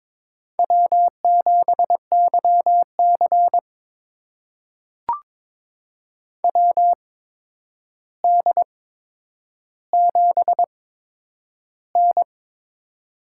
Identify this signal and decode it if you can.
Morse code, heard as a keyed tone, about 22 wpm, spaced as for 9 wpm, 700 Hz. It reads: W7YC WD7N
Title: USA Call Signs - 4 Characters - 8x Character Spacing - 22wpm